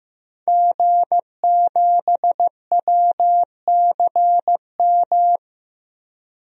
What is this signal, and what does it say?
Morse code, heard as a keyed tone, about 15 wpm, 700 Hz. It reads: G7WCM
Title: DXCC-100 Call Signs - 5 Characters - 15wpm